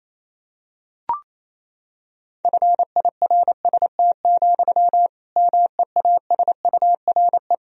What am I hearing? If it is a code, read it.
Morse code, heard as a keyed tone, about 28 wpm, 700 Hz. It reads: FIRST, MEASURE
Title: Sentences from Top 1000 Words - Review Part 3 - 28wpm